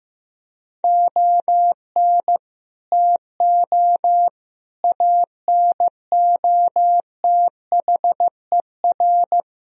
Morse code, heard as a keyed tone, about 15 wpm, 700 Hz. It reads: ON TO ANOTHER
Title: Sentences from Top 1000 Words - Review Part 2 - 15wpm